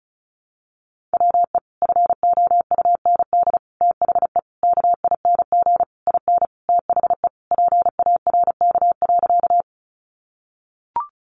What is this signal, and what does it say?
Morse code, heard as a keyed tone, about 34 wpm, 700 Hz. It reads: WE FOUND THE KING IN THE PARK.
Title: Sentences from Top 1300 Words 35wpm